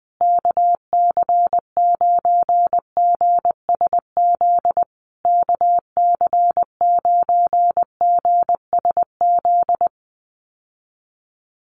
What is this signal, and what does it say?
Morse code, heard as a keyed tone, about 20 wpm, 700 Hz. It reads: KC9GSZ KC9GSZ
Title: USA Call Signs Repeated 3x 20wpm